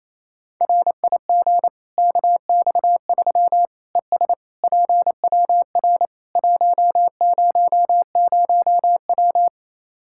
Morse code, heard as a keyed tone, about 28 wpm, 700 Hz. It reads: RIG KX3 ES PWR 100W
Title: Practice QSO - Exchange 2 - Info - 28wpm